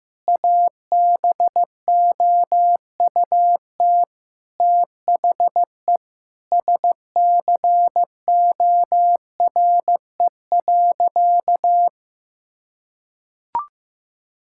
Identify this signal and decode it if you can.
Morse code, heard as a keyed tone, about 15 wpm, 700 Hz. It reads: ABOUT THE SCORE.
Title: Sentences from Top 1600 Words 15wpm